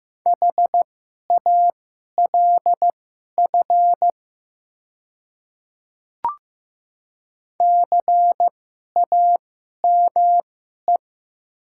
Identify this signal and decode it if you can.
Morse code, heard as a keyed tone, about 15 wpm, spaced as for 11 wpm, 700 Hz. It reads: HALF CAME
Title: 2x Character Spacing - Top 500 Words - 15wpm